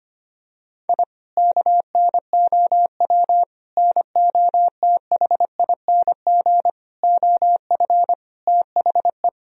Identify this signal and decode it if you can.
Morse code, heard as a keyed tone, about 25 wpm, 700 Hz. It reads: I KNOW NOTHING OF THE
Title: Sentences from Top 1000 Words - Review Part 3 - 25wpm